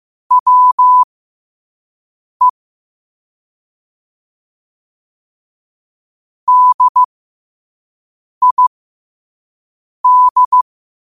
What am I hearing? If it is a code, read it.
Morse code, heard as a keyed tone, about 15 wpm, spaced as for 5 wpm, 1000 Hz. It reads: WE DID
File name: qotdfiles / 5wpm